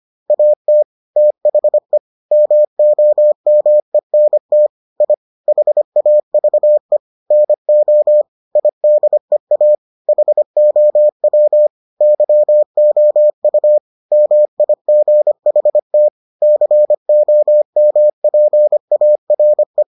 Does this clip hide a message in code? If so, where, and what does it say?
Yes: Morse code from 0.3 to 19.8 s, heard as a keyed tone, about 25 wpm, 600 Hz. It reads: AT THE MOMENT I HAVE NO IDEA HOW YOU MIGHT COMPARE